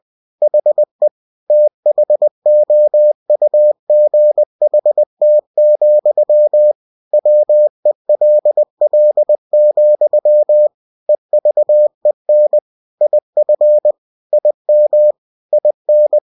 Morse code, heard as a keyed tone, about 20 wpm, 600 Hz. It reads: HE THOUGHT, WELL, EVEN IF IM IN
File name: cw-books / winnie-the-pooh / winnie-20wpm-page-0053